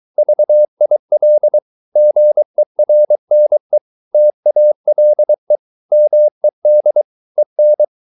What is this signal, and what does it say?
Morse code, heard as a keyed tone, about 23 wpm, 600 Hz. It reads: VIL GERNE TALE MED EN
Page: Ferie 23 wpm